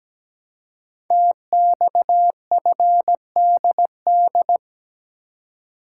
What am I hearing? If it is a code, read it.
Morse code, heard as a keyed tone, about 17 wpm, 700 Hz. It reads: TXFDD
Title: Code Groups - 5 Letters - 17wpm